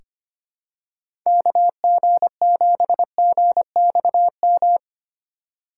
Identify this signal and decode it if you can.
Morse code, heard as a keyed tone, about 25 wpm, 700 Hz. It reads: KG7GXM
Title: USA Call Signs - 4x Character Spacing - 25wpm